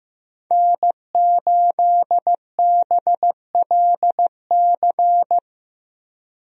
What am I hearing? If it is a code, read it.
Morse code, heard as a keyed tone, about 15 wpm, 700 Hz. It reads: N8BLC
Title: USA Call Signs - 5 Characters - Repeated 2x - 15wpm